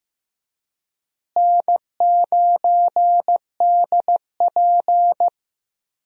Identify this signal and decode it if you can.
Morse code, heard as a keyed tone, about 15 wpm, 700 Hz. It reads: N9DP
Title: USA Call Signs - 4 Characters - Repeated 3x - 15wpm